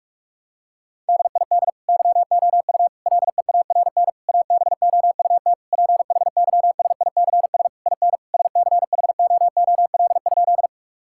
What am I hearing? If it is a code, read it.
Morse code, heard as a keyed tone, about 46 wpm, 700 Hz. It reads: DID YOU LEARN ABOUT PHYSICS IN SCHOOL?
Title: Sentences from Top 1000 Words - Review Part 5 - 45wpm